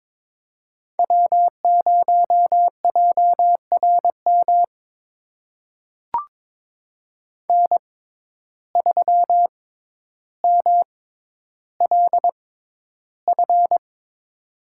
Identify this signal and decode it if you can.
Morse code, heard as a keyed tone, about 22 wpm, spaced as for 11 wpm, 700 Hz. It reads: W0JRM N3MLF
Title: USA Call Signs - 5 Characters - 6x Character Spacing - 22wpm